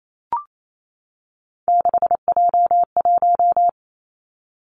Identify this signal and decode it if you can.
Morse code, heard as a keyed tone, about 28 wpm, 700 Hz. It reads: 6J1
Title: Three Challenging Letter-Number Combinations 28wpm